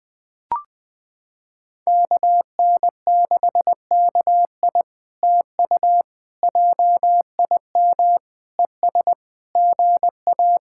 Morse code, heard as a keyed tone, about 20 wpm, 700 Hz. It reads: KN6KI TU JIM ES GA
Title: POTA QSO Practice - 20wpm